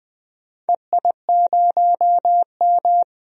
Morse code, heard as a keyed tone, about 20 wpm, 700 Hz. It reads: EI0M
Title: DXCC-100 Call Signs - 4 Characters - 20wpm